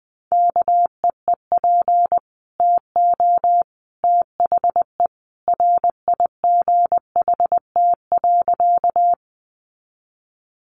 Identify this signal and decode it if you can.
Morse code, heard as a keyed tone, about 20 wpm, 700 Hz. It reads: KEEP TO THE RIGHT.
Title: Sentences from Top 300 Words 20wpm